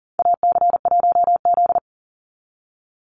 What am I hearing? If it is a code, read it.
Morse code, heard as a keyed tone, about 37 wpm, 700 Hz. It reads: AC1Z
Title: NA SOTA Chaser Callsigns 40wpm